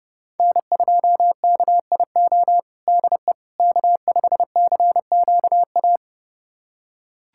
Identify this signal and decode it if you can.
Morse code, heard as a keyed tone, about 32 wpm, 700 Hz. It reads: N2KIO DE K5CQA
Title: USA Callsign DE Callsign - 5 Characters - 30wpm